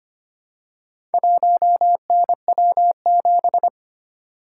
Morse code, heard as a keyed tone, about 25 wpm, 700 Hz. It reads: 1NW7
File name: practice / 25wpm / Four Letter-Number Combinations - 25wpm